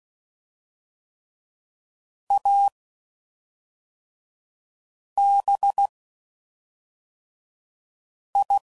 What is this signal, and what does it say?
Morse code, heard as a keyed tone, about 16 wpm, spaced as for 3 wpm, 800 Hz. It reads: ABI
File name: Suffixes / Special INGs / Fast INGs